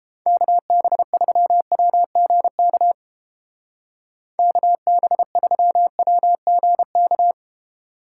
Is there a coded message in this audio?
Yes: Morse code from 0.3 to 7.3 s, heard as a keyed tone, about 29 wpm, 700 Hz. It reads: KB3WGK KB3WGK